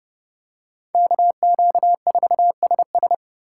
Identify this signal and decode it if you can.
Morse code, heard as a keyed tone, about 27 wpm, 700 Hz. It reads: KQ4SS